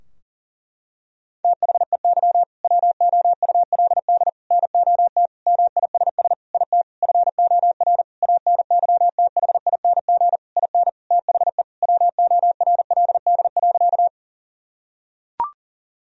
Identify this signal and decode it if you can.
Morse code, heard as a keyed tone, about 40 wpm, 700 Hz. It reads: THEY WOULD NOT MISS IT FOR ANYTHING IN THE WORLD.
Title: Sentences from Top 1000 Words - Review Part 3 - 40wpm